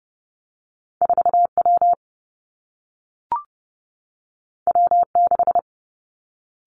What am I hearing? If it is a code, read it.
Morse code, heard as a keyed tone, about 30 wpm, 700 Hz. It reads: 4 W W 6
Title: Two Challenging Letter-Number Combinations 30wpm